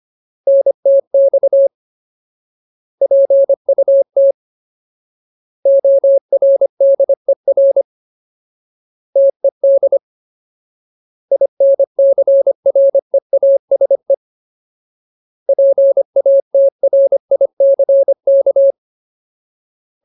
Morse code, heard as a keyed tone, about 25 wpm, 550 Hz. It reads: NTX PUT ORDER TED INCREASE PATRICK
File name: Ultimate Word List / 25WPM Ultimate